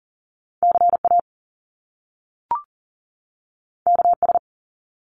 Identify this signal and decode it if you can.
Morse code, heard as a keyed tone, about 42 wpm, 700 Hz. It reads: CA KS
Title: US States 40wpm